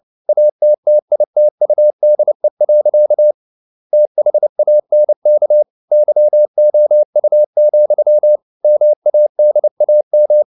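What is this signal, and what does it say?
Morse code, heard as a keyed tone, about 29 wpm, 600 Hz. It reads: ATTITUDE. THANK YOU, MADAM
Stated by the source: Morse code at 29 wpm, sending Ken Follett's novel Never in English, from CW med Gnister.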